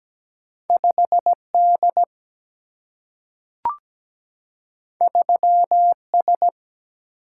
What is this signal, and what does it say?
Morse code, heard as a keyed tone, about 17 wpm, 700 Hz. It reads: 5D 3S
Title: Single-Letter Single-Number 17wpm